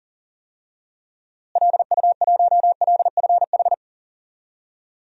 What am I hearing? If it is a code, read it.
Morse code, heard as a keyed tone, about 40 wpm, 700 Hz. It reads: LU1LFH
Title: DXCC-100 Call Signs - 6 Characters - Repeated 3x - 40wpm